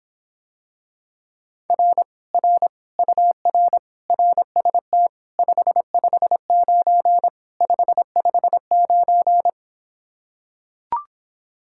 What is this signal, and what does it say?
Morse code, heard as a keyed tone, about 26 wpm, 700 Hz. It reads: R R UR RST 559 559